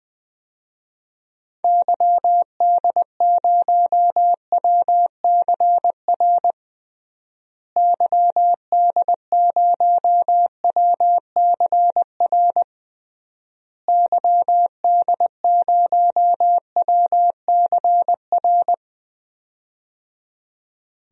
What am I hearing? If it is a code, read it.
Morse code, heard as a keyed tone, about 21 wpm, 700 Hz. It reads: YD0WCR YD0WCR YD0WCR
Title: DXCC-100 Call Signs - 6 Characters - Repeated 3x - 20wpm